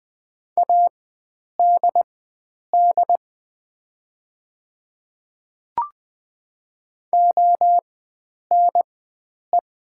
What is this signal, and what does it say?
Morse code, heard as a keyed tone, about 20 wpm, spaced as for 9 wpm, 700 Hz. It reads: ADD ONE
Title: 4x Character Spacing - Top 500 Words - 20wpm